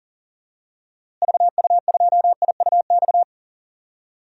Morse code, heard as a keyed tone, about 40 wpm, 700 Hz. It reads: VU2IUX
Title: DXCC-100 Call Signs - 6 Characters - Repeated 2x - 40wpm